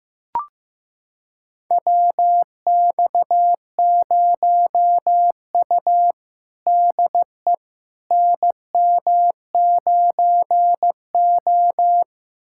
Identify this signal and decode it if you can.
Morse code, heard as a keyed tone, about 15 wpm, 700 Hz. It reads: WX0U DE NM9O
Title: USA Callsign DE Callsign - 4 Characters - 15wpm